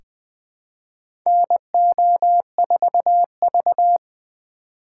Morse code, heard as a keyed tone, about 20 wpm, 700 Hz. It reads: NO4V
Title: USA Call Signs - 4 Characters - 3x Character Spacing - 20wpm